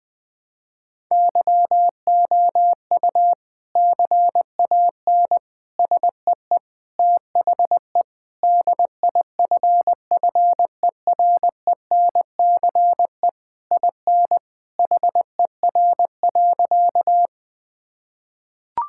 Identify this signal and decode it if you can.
Morse code, heard as a keyed tone, about 20 wpm, 700 Hz. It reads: YOU CAN SEE THE DIFFERENCE IN HER.
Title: Sentences from Top 1000 Words - Review Part 5 - 20wpm